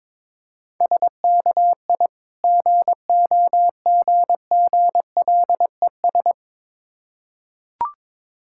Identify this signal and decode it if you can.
Morse code, heard as a keyed tone, about 22 wpm, 690 Hz. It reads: SKI GOGGLES
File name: Difficult Words with Hints - 22wpm